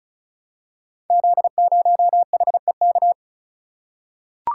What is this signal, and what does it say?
Morse code, heard as a keyed tone, about 36 wpm, 700 Hz. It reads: Z0HEK
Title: Code Groups - 5 Letter:Numbers - 35wpm